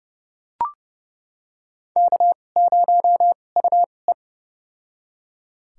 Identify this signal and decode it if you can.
Morse code, heard as a keyed tone, about 29 wpm, spaced as for 21 wpm, 700 Hz. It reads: K0UE
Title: USA Call Signs - 4 Characters - 2x Character Spacing - 30wpm